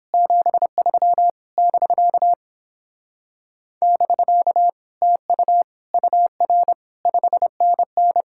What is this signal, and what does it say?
Morse code, heard as a keyed tone, about 31 wpm, 700 Hz. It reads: 73 <BK> <BK> TU UR 5NN